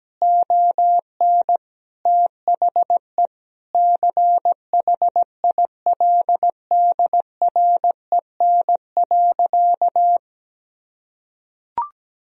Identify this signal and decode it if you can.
Morse code, heard as a keyed tone, about 17 wpm, 700 Hz. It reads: ON THE CHILDREN.
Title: Sentences from Top 300 Words and Abbreviations 17wpm